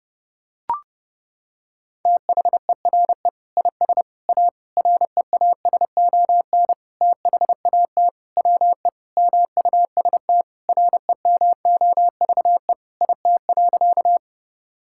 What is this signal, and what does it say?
Morse code, heard as a keyed tone, about 30 wpm, 700 Hz. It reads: THERE IS A REASON THAT WE MUST REMOVE IT.
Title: Sentences from Top 1000 Words - Review Part 5 - 30wpm